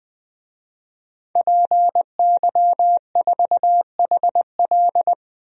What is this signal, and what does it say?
Morse code, heard as a keyed tone, about 21 wpm, 700 Hz. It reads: PY4HL
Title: DXCC-100 Call Signs - 5 Characters - 20wpm